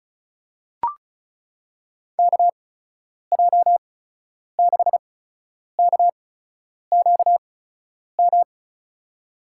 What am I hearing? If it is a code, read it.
Morse code, heard as a keyed tone, about 36 wpm, spaced as for 10 wpm, 700 Hz. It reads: KJ6KQM
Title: USA Call Signs - 8x Character Spacing - 35wpm